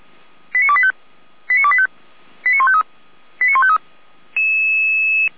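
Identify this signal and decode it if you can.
Five-tone selective call (ZVEI), heard as quick runs of five tones, tones of about 70 ms. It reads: 89285 89285 89153 89153
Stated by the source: ZVEI five-tone selective call 89338, received via SDR.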